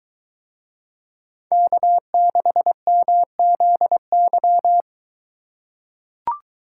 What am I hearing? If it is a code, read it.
Morse code, heard as a keyed tone, about 22 wpm, 700 Hz. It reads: K6MZY